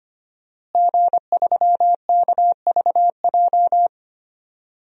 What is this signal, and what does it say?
Morse code, heard as a keyed tone, about 25 wpm, 700 Hz. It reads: G3KVJ
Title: DXCC-100 Call Signs - 5 Characters - Repeated 2x - 25wpm